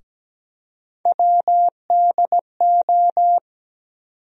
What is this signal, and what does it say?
Morse code, heard as a keyed tone, about 17 wpm, 700 Hz. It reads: WDO
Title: USA Call Sign Postfixes - 3 Letters 17wpm